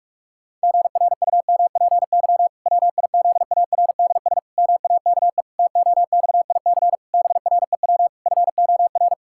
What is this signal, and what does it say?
Morse code, heard as a keyed tone, about 46 wpm, 700 Hz. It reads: GRUMPY WIZARDS MAKE TOXIC BREW FOR
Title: Pangram Challenge v2 - 45wpm